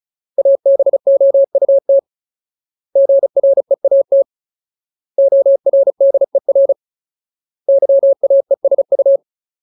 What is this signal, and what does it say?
Morse code, heard as a keyed tone, about 34 wpm, 550 Hz. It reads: ABOUT GREAT ORDER YAESU
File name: Ultimate Word List / 35WPM Ultimate